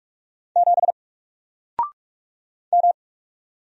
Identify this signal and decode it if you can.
Morse code, heard as a keyed tone, about 44 wpm, 700 Hz. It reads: 7 M
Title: Single Letter-Number 45wpm